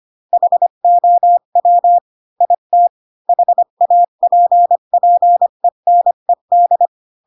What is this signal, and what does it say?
Morse code, heard as a keyed tone, about 25 wpm, 700 Hz. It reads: HOW IT HAPPENED
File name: Pinocchio / Pinocchio ch1 20 25 ik1hge